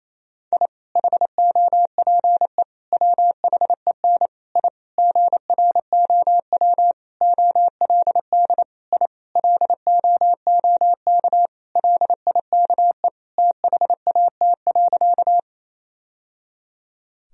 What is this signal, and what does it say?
Morse code, heard as a keyed tone, about 28 wpm, 700 Hz. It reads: I HOPE WHEN I GROW OLD I LOOK LIKE THAT.
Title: Sentences from Top 1000 Words - Review Part 1 - 28wpm